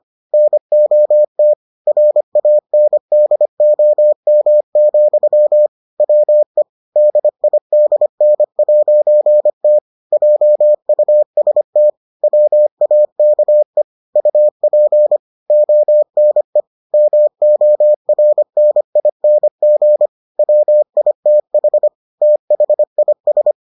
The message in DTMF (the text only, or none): none